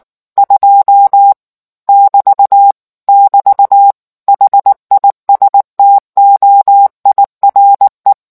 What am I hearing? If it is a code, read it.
Morse code, heard as a keyed tone, about 19 wpm, 800 Hz. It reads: 2 <BT> <BT> HISTOIRE